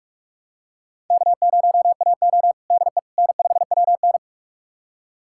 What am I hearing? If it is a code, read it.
Morse code, heard as a keyed tone, about 46 wpm, 700 Hz. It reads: K0AO DE N5WN
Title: USA Callsign DE Callsign - 4 Characters - 45wpm